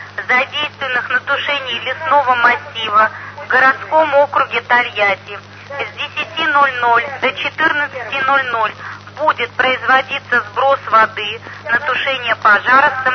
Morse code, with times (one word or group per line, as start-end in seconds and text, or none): none